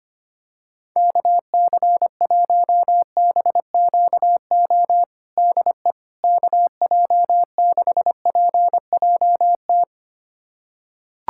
Morse code, heard as a keyed tone, about 25 wpm, 700 Hz. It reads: KC1BQO DE KJ6PJT
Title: USA Callsign DE Callsign - 6 Characters - 25wpm